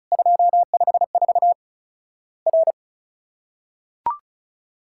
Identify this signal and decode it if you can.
Morse code, heard as a keyed tone, about 34 wpm, 700 Hz. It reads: 254 R
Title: Worked All Europe DX Contest - QTC Exchange - 35wpm